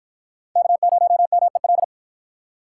Morse code, heard as a keyed tone, about 52 wpm, 700 Hz. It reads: K0MEL